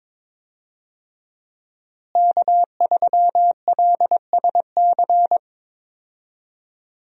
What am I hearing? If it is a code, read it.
Morse code, heard as a keyed tone, about 22 wpm, 700 Hz. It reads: K3LSC
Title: USA Call Signs - 5 Characters - Repeated 3x - 22wpm